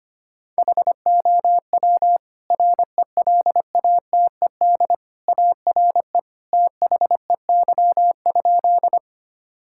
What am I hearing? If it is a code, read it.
Morse code, heard as a keyed tone, about 25 wpm, 700 Hz. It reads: HOW RELATED ARE THEY?
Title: Sentences from Top 1000 Words - Review Part 5 - 25wpm